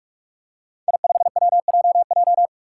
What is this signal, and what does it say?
Morse code, heard as a keyed tone, about 46 wpm, 700 Hz. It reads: I5WJJ
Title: Five Challenging Letter-Number Combinations 45wpm